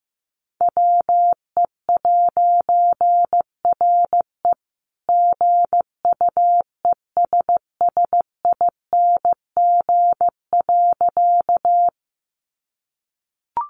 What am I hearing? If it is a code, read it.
Morse code, heard as a keyed tone, about 15 wpm, 700 Hz. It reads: WE'RE GUESSING.
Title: A Sound Of Thunder 15wpm